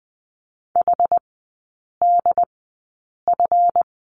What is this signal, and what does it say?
Morse code, heard as a keyed tone, about 20 wpm, spaced as for 8 wpm, 700 Hz. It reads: HDF
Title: 152 Lesson 25e - Letter p - Three Characters 20wpm